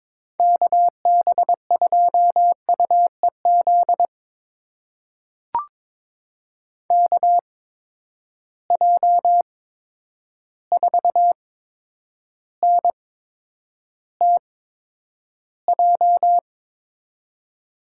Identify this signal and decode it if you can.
Morse code, heard as a keyed tone, about 23 wpm, spaced as for 10 wpm, 700 Hz. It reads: KB2UEZ KJ4NTJ
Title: USA Call Signs - 6 Characters - 8x Character Spacing - 22wpm